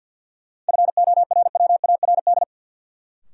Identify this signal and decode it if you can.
Morse code, heard as a keyed tone, about 48 wpm, 700 Hz. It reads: FORWARD
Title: Sentences from Top 1300 Words - New Word Drill - 50wpm